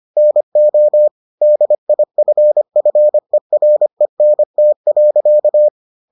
Morse code, heard as a keyed tone, about 25 wpm, 600 Hz. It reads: NO DIFFERENT.